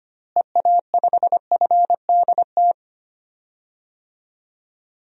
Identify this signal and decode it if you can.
Morse code, heard as a keyed tone, about 25 wpm, 700 Hz. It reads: EA5FDT
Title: DXCC-100 Call Signs - 6 Characters - Repeated 2x - 25wpm